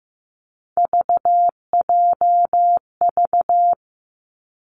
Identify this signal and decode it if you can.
Morse code, heard as a keyed tone, about 15 wpm, 700 Hz. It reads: VJV
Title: Three Challenging Letter-Number Combinations 15wpm